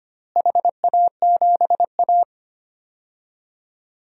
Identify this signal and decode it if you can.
Morse code, heard as a keyed tone, about 25 wpm, 700 Hz. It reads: HA7A
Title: DXCC-100 Call Signs - 4 Characters - Repeated 2x - 25wpm